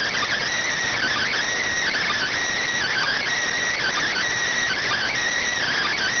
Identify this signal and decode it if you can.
Emergency Alert System (SAME) tones (header bursts), heard as harsh buzzing bursts. It is heard from 0.9 to 5.1 s.